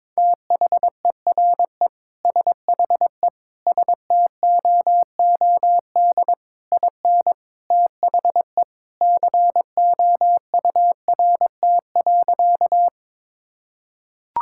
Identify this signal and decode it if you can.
Morse code, heard as a keyed tone, about 23 wpm, 700 Hz. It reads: THERE SHE STOOD IN THE COURT.
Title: Sentences from Top 1600 Words 22wpm